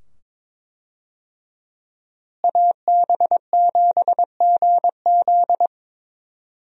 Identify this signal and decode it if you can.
Morse code, heard as a keyed tone, about 23 wpm, 700 Hz. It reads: AB7GZ
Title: USA Call Signs - 5 Characters - Repeated 2x - 22wpm